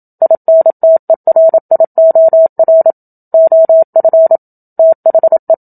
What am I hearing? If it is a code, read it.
Morse code, heard as a keyed tone, about 27 wpm, 650 Hz. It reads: INTERIOR OF THE
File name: Book Two / 01-Chapter 1 - 27 WPM